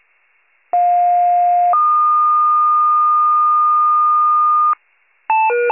Two-tone page, heard as a one-second tone, then a three-second tone, about 710 then 1220 Hz. It is heard from 0.7 to 4.7 s.